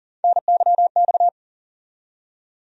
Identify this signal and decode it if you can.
Morse code, heard as a keyed tone, about 40 wpm, 700 Hz. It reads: NYX
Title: USA Call Sign Postfixes - 3 Letters 40wpm